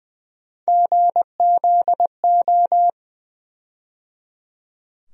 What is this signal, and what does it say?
Morse code, heard as a keyed tone, about 20 wpm, 700 Hz. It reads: GZO
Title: USA Call Sign Postfixes - 3 Letters 20wpm